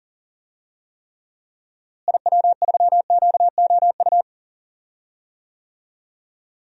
Morse code, heard as a keyed tone, about 40 wpm, 700 Hz. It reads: IW3QOU
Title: DXCC-100 Call Signs - 6 Characters - Repeated 3x - 40wpm